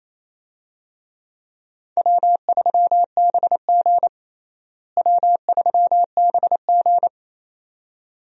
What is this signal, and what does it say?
Morse code, heard as a keyed tone, about 28 wpm, 700 Hz. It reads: W3BG W3BG
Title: USA Call Signs - 4 Characters - Repeated 2x - 28wpm